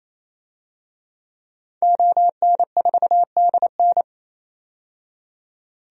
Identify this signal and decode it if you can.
Morse code, heard as a keyed tone, about 28 wpm, 700 Hz. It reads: ON4DN
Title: DXCC-100 Call Signs - 5 Characters - 28wpm